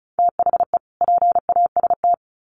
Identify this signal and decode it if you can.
Morse code, heard as a keyed tone, about 34 wpm, 700 Hz. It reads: THE PAST
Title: Sets of 2 Words 35wpm